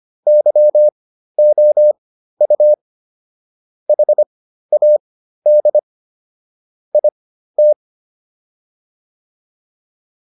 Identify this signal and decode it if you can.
Morse code, heard as a keyed tone, about 25 wpm, spaced as for 13 wpm, 600 Hz. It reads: YOU HAD IT